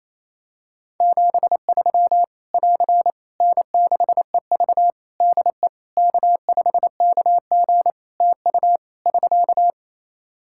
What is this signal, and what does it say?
Morse code, heard as a keyed tone, about 27 wpm, 700 Hz. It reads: 73 <AR> N6EV DE K5KG TU <SK>
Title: K1USN SST QSO Practice 28wpm